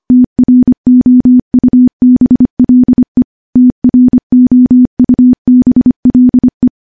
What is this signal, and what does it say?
Morse code, heard as a keyed tone, about 25 wpm, 250 Hz. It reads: TROUBLE TROUBLE
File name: morse-training-25WPM